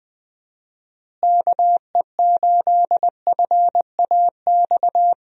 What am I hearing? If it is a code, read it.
Morse code, heard as a keyed tone, about 20 wpm, 700 Hz. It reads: KE8FAX
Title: USA Call Signs - 6 Characters - Repeated 3x - 20wpm